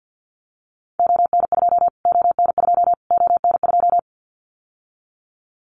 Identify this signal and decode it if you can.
Morse code, heard as a keyed tone, about 50 wpm, 700 Hz. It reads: ON2 ON2 ON2
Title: Callsign Prefixes for Top 25 DXCC - Repeated 3x - 50wpm